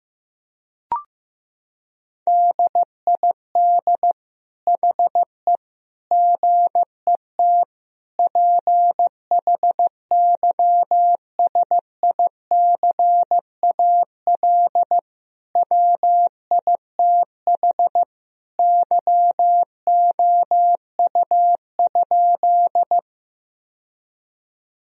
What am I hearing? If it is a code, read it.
Morse code, heard as a keyed tone, about 15 wpm, 700 Hz. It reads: DID HE GET PHYSICAL WITH YOU?
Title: Sentences from Top 1000 Words - Review Part 4 - 15wpm